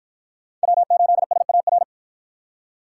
Morse code, heard as a keyed tone, about 52 wpm, 700 Hz. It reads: W8SAR